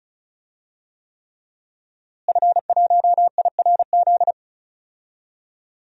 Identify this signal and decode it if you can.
Morse code, heard as a keyed tone, about 34 wpm, 700 Hz. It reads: F1IRZ
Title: DXCC-100 Call Signs - 5 Characters - Repeated 3x - 35wpm